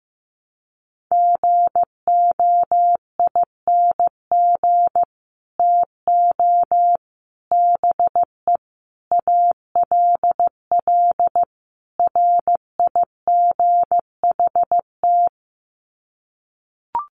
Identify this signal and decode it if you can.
Morse code, heard as a keyed tone, about 15 wpm, 700 Hz. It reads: GOING TO BE ALL RIGHT
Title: Sets of 5 Words Encore2 15wpm